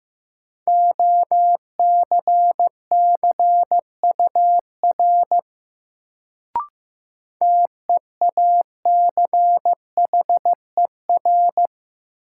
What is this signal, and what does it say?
Morse code, heard as a keyed tone, about 15 wpm, 700 Hz. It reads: OCCUR TEACHER
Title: Sentences from Top 900 Words - New Word Drill - 15wpm